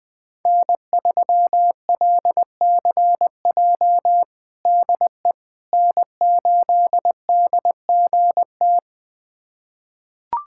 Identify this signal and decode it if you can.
Morse code, heard as a keyed tone, about 20 wpm, 700 Hz. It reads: N3LCJ DE N8DGT
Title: USA Callsign DE Callsign 20wpm